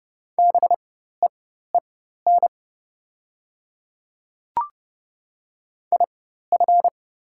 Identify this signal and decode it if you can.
Morse code, heard as a keyed tone, about 30 wpm, spaced as for 14 wpm, 700 Hz. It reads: BEEN IF
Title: 4x Character Spacing - Top 100 Words - 30wpm